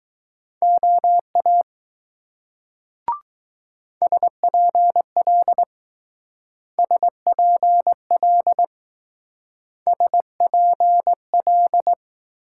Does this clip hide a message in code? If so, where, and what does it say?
Yes: Morse code from 0.6 to 11.9 s, heard as a keyed tone, about 19 wpm, 700 Hz. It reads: OA SPL SPL SPL